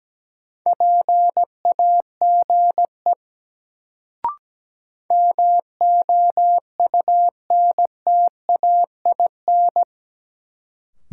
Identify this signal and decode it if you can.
Morse code, heard as a keyed tone, about 17 wpm, 700 Hz. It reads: PAGE MOUNTAIN
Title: Sentences from Top 1000 Words - Review Part 1 - New Word Drill - 17wpm